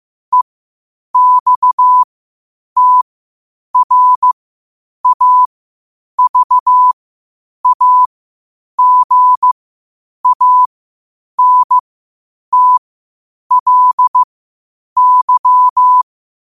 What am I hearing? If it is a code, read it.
Morse code, heard as a keyed tone, about 15 wpm, spaced as for 8 wpm, 1000 Hz. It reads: EXTRAVAGANTLY
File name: qotdfiles / 8wpm